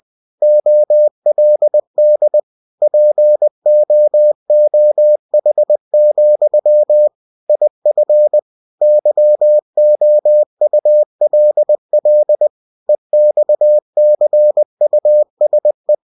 Morse code, heard as a keyed tone, about 20 wpm, 600 Hz. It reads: OLD POOH, IF YOULL EXCUSE
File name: cw-books / winnie-the-pooh / winnie-20wpm-page-0024